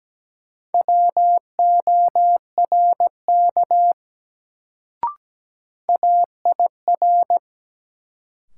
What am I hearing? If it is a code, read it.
Morse code, heard as a keyed tone, about 17 wpm, 700 Hz. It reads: WORK AIR
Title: Sentences from Top 200 Words - New Word Drill - 17wpm